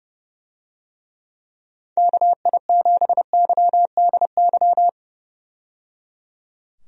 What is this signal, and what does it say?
Morse code, heard as a keyed tone, about 30 wpm, 700 Hz. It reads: KI7YDY
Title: USA Call Signs - 6 Characters - 30wpm